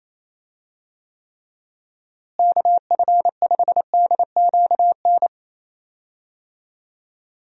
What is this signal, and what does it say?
Morse code, heard as a keyed tone, about 27 wpm, 700 Hz. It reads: KF5DQN
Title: USA Call Signs - 6 Characters - Repeated 2x - 28wpm